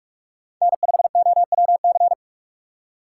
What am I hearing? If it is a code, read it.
Morse code, heard as a keyed tone, about 44 wpm, 700 Hz. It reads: N5OWC